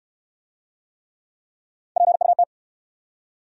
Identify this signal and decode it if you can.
Morse code, heard as a keyed tone, about 65 wpm, 700 Hz. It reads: PUT